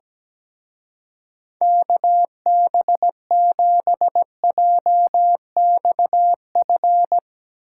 Morse code, heard as a keyed tone, about 17 wpm, 700 Hz. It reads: KB7JXF
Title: Speed Racing - USA Call Signs 17wpm